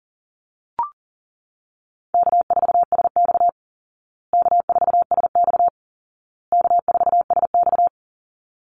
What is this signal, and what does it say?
Morse code, heard as a keyed tone, about 40 wpm, 700 Hz. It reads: K4SX K4SX K4SX
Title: Easy Call Signs Repeated 3x - 40wpm